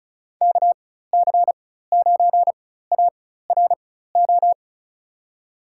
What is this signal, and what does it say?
Morse code, heard as a keyed tone, about 36 wpm, spaced as for 16 wpm, 700 Hz. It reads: KC9ARO
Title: USA Call Signs - 4x Character Spacing - 35wpm